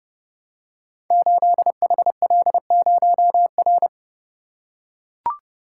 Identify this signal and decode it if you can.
Morse code, heard as a keyed tone, about 30 wpm, 700 Hz. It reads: 8HL0R
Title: Code Groups - 5 Letter:Numbers - 30wpm